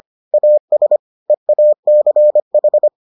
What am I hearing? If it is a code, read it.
Morse code, heard as a keyed tone, about 25 wpm, 600 Hz. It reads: AS EACH